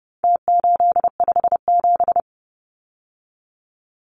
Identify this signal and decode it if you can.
Morse code, heard as a keyed tone, about 30 wpm, 700 Hz. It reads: T857
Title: Numbers - Contest Style - 4-Digit 30wpm